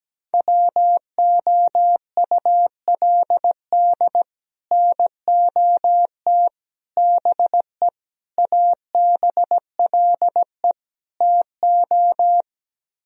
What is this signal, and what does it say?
Morse code, heard as a keyed tone, about 17 wpm, 700 Hz. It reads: WOULD NOT BE ABLE TO
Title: Speed-Racing - Sets of 5 Words 17wpm